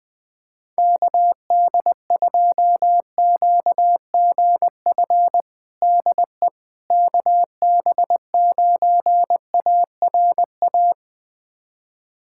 Morse code, heard as a keyed tone, about 20 wpm, 700 Hz. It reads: KD2QGF DE KB9ARA
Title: USA Callsign DE Callsign - 6 Characters - 20wpm